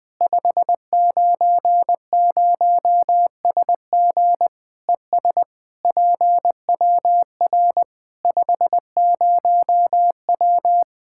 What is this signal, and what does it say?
Morse code, heard as a keyed tone, about 20 wpm, 700 Hz. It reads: 590SG ES PWR 50W
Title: K1USN SST QSO Practice 20wpm